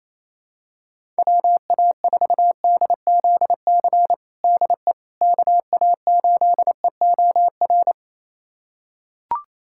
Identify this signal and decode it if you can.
Morse code, heard as a keyed tone, about 28 wpm, 700 Hz. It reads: WA4DZC DE KA8EOR
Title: USA Callsign DE Callsign - 6 Characters - 28wpm